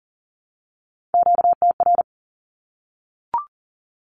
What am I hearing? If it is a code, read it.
Morse code, heard as a keyed tone, about 40 wpm, 700 Hz. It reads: QTR
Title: Q-Codes Repeated 3x 40wpm